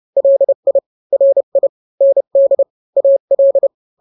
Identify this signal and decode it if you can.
Morse code, heard as a keyed tone, about 30 wpm, 550 Hz. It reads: LI RI ND AL
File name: n-grams / 2-grams 30 25 WPM